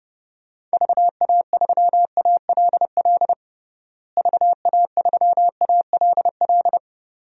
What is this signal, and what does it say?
Morse code, heard as a keyed tone, about 32 wpm, 700 Hz. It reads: VA3ALL VA3ALL
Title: DXCC-100 Call Signs - 6 Characters - Repeated 2x - 30wpm